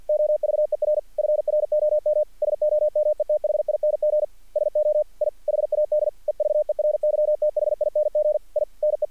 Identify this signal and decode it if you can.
Morse code, heard as a keyed tone, about 50 wpm, 600 Hz. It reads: OVER FROM SOMETHING SO I HAD EVERYTHING I NE
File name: archived / CRSnet